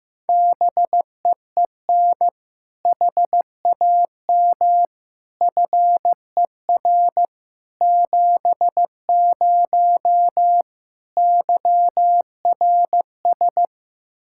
Morse code, heard as a keyed tone, about 15 wpm, 700 Hz. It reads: BEEN HAM FER 70 YRS
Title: Practice QSO - Exchange 3 - More Info - 15wpm